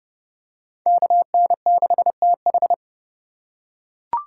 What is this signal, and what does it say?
Morse code, heard as a keyed tone, about 27 wpm, 700 Hz. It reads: KN6TH